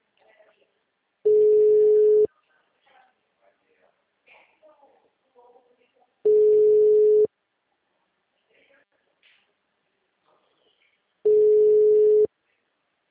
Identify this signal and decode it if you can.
Ringback tone (Europe, 425 Hz), with speech.